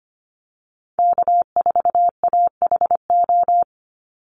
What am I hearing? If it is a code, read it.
Morse code, heard as a keyed tone, about 25 wpm, 700 Hz. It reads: K4AHO
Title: Call Signs - 25wpm